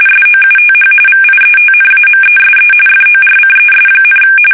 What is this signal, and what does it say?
RTTY, about 45.45 baud. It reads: SOUTH VIETNAMESE FORCES,
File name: fsk850 / battleshipnj